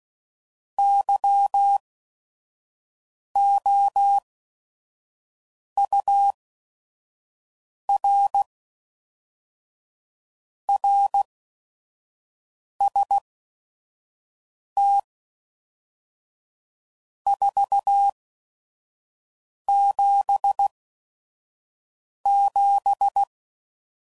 Morse code, heard as a keyed tone, about 16 wpm, spaced as for 5 wpm, 800 Hz. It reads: YOUR RST 477
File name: CW Code Groups @5WPM / RST's / Your RST's